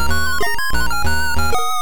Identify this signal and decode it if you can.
There is no code.